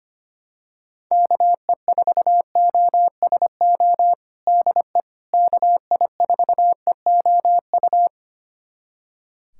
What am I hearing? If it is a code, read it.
Morse code, heard as a keyed tone, about 25 wpm, 700 Hz. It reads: KE4OSO DE KI4EOU
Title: USA Callsign DE Callsign - 6 Characters - 25wpm